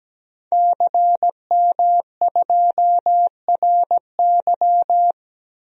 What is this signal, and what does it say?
Morse code, heard as a keyed tone, about 17 wpm, 700 Hz. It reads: CM2RY
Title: DXCC-100 Call Signs - 5 Characters - 17wpm